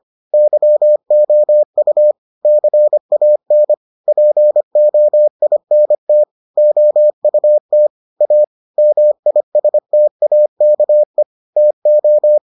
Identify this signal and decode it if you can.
Morse code, heard as a keyed tone, about 25 wpm, 600 Hz. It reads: YOU CAN POINT OUT A MISTAKE TO